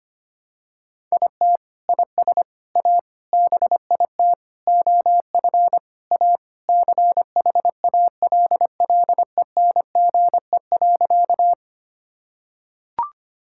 Morse code, heard as a keyed tone, about 25 wpm, 700 Hz. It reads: IT IS A BIT OF A CHALLENGE.
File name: practice / 25wpm / Sentences with 4-Word Phrases - 25wpm